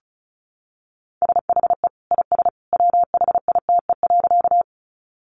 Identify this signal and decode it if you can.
Morse code, heard as a keyed tone, about 34 wpm, 700 Hz. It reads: SHE IS WHITE.
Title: Easy Phrases 35wpm